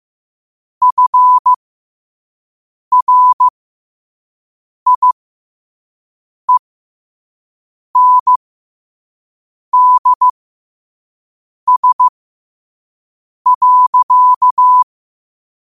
Morse code, heard as a keyed tone, about 15 wpm, spaced as for 5 wpm, 1000 Hz. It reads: FRIENDS.
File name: qotdfiles / 5wpm